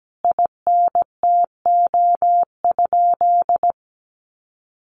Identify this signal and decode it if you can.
Morse code, heard as a keyed tone, about 17 wpm, 700 Hz. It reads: INTO?
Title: Sentences from Top 600 Words 17wpm